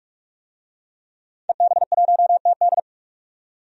Morse code, heard as a keyed tone, about 46 wpm, 700 Hz. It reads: EB1TD
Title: DXCC-100 Call Signs - 5 Characters - 45wpm